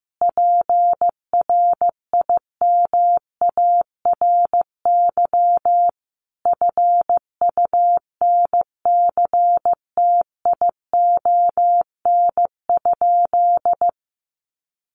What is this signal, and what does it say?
Morse code, heard as a keyed tone, about 15 wpm, 700 Hz. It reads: PRIMARY FUNCTION?
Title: Sentences from Top 1100 Words 15wpm